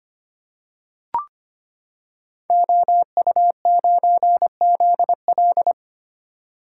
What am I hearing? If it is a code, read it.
Morse code, heard as a keyed tone, about 25 wpm, 700 Hz. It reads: OU9ZL
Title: Code Groups - 5 Letter:Numbers - 25wpm